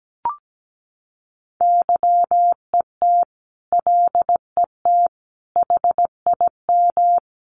Morse code, heard as a keyed tone, about 17 wpm, 700 Hz. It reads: YET LET HIM
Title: Sentences from Top 1500 Words 17wpm